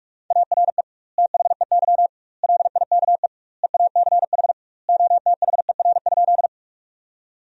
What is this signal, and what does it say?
Morse code, heard as a keyed tone, about 46 wpm, 700 Hz. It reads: ARE THEY LIKE EACH OTHER?
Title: 2x Word-Spacing - Sentences from Top 200 Words - 45wpm